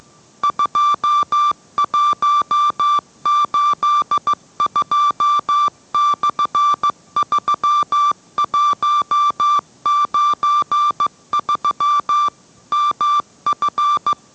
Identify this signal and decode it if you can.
Morse code, heard as a keyed tone, about 15 wpm, 1170 Hz. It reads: 2182/3193 MF